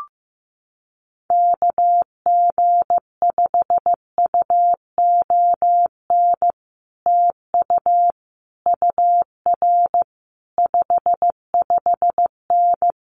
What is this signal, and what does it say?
Morse code, heard as a keyed tone, about 15 wpm, 700 Hz. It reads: KG5UON TU UR 55N
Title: Common QSO Phrases Encore 15wpm